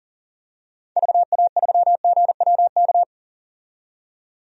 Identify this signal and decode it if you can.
Morse code, heard as a keyed tone, about 40 wpm, 700 Hz. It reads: VA3GWK
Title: DXCC-100 Call Signs - 6 Characters - 40wpm